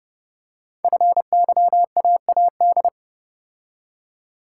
Morse code, heard as a keyed tone, about 30 wpm, 690 Hz. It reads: FYAAD
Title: Code Groups - 5 Letters - 30wpm